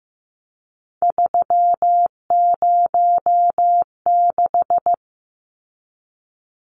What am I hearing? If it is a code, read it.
Morse code, heard as a keyed tone, about 15 wpm, 700 Hz. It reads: 306
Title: Numbers - 3-Digit 15wpm